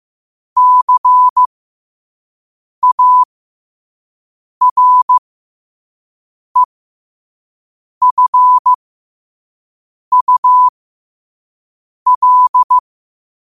Morse code, heard as a keyed tone, about 15 wpm, spaced as for 5 wpm, 1000 Hz. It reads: CAREFUL
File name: qotdfiles / 5wpm